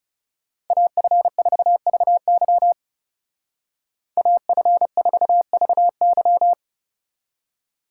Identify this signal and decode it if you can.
Morse code, heard as a keyed tone, about 34 wpm, 700 Hz. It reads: AF4VY AF4VY
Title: Speed Racing - USA 2x2 Call Signs 30wpm